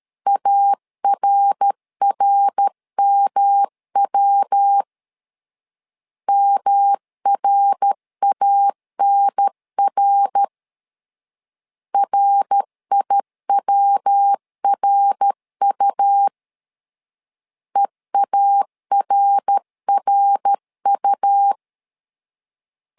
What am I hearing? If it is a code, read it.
Morse code, heard as a keyed tone, about 13 wpm, 800 Hz. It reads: ARRMW MRANR RIWRU EARRU
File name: LEZIONI / 3 MWRO / 3 MWR 12wpm 1711